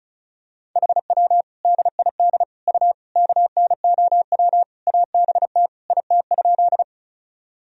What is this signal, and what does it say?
Morse code, heard as a keyed tone, about 36 wpm, 700 Hz. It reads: HW DID U KNOW ABT IT?
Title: Sentences from Top 100 Words and Abbreviations 35wpm